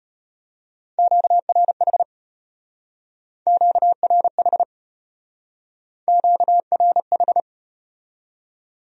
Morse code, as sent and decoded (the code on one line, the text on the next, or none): --.- .-. .... / --.- .-. .... / --.- .-. ....
QRH QRH QRH